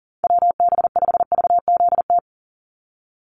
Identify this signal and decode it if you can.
Morse code, heard as a keyed tone, about 40 wpm, 700 Hz. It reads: WB5VZT
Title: USA Call Signs 40wpm